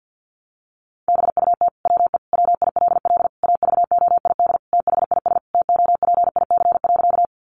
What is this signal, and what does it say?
Morse code, heard as a keyed tone, about 50 wpm, 700 Hz. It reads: BUT WE WILL AVOID THIS TOPIC.
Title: Sentences from Top 1200 Words 50wpm